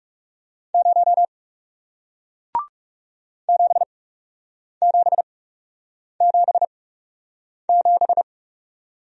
Morse code, heard as a keyed tone, about 39 wpm, spaced as for 8 wpm, 700 Hz. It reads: 0 7777